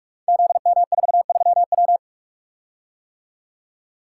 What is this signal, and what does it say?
Morse code, heard as a keyed tone, about 44 wpm, 700 Hz. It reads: ZM43W
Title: Code Groups - 5 Letter:Numbers - 45wpm